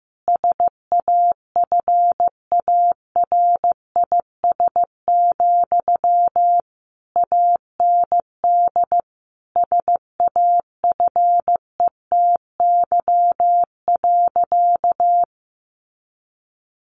Morse code, heard as a keyed tone, about 15 wpm, 700 Hz. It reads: SAFARIS, AND SAFETY.
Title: A Sound Of Thunder 15wpm